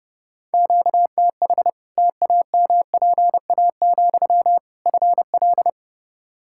Morse code, heard as a keyed tone, about 31 wpm, 700 Hz. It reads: QTH TAMPA, FL
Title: Practice QSO - Exchange 1 - Beginning - 30wpm